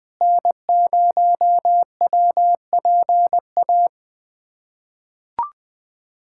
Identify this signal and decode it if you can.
Morse code, heard as a keyed tone, about 19 wpm, 700 Hz. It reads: N0WPA
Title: USA Callsign DE Callsign 20wpm